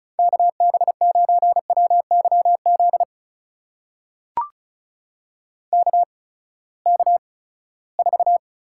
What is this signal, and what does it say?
Morse code, heard as a keyed tone, about 36 wpm, spaced as for 20 wpm, 700 Hz. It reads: KB9WYZ KK4
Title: USA Call Signs - 6 Characters - 8x Character Spacing - 35wpm